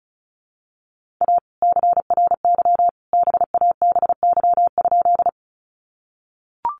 Morse code, heard as a keyed tone, about 34 wpm, 700 Hz. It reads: A CRY BABY?
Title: Sentences from Top 1300 Words 35wpm